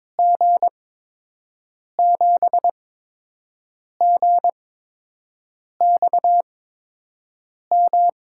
Morse code, heard as a keyed tone, about 21 wpm, spaced as for 6 wpm, 700 Hz. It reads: G7GXM
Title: USA Call Signs - 6 Characters - 8x Character Spacing - 22wpm